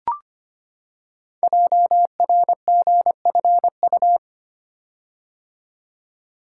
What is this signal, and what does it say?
Morse code, heard as a keyed tone, about 25 wpm, 700 Hz. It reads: JRGFU
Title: Code Groups - 5 Letters - 25wpm